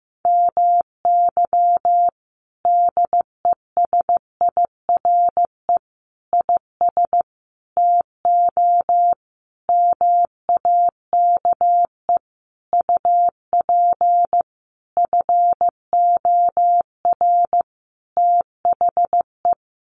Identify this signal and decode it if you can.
Morse code, heard as a keyed tone, about 15 wpm, 700 Hz. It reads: MY DESIRE IS TO MAKE UP FOR THE
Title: Sentences from Top 1100 Words 15wpm